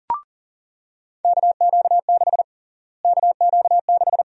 Morse code, heard as a keyed tone, about 40 wpm, 700 Hz. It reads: KQ6 KQ6
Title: USA Call Sign Prefixes - Repeated 2x - 40wpm